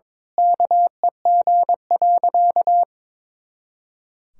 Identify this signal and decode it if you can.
Morse code, heard as a keyed tone, about 22 wpm, 700 Hz. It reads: KEG.
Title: Pangram Challenge v2 - 22wpm